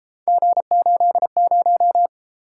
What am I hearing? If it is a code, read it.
Morse code, heard as a keyed tone, about 32 wpm, 700 Hz. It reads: G80